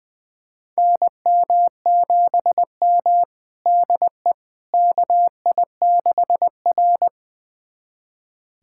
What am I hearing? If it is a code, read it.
Morse code, heard as a keyed tone, about 21 wpm, 700 Hz. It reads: NM7M DE KI6R
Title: USA Callsign DE Callsign - 4 Characters - 20wpm